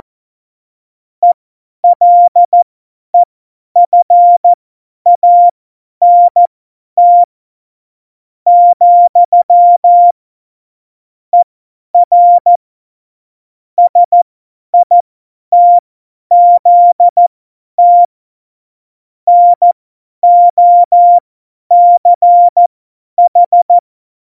Morse code, heard as a keyed tone, about 14 wpm, spaced as for 10 wpm, 700 Hz. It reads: ELEFANT , ER SITZT NOCH
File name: Morsekurs / Witze / Witz 23. Oktober 2018, 50 F70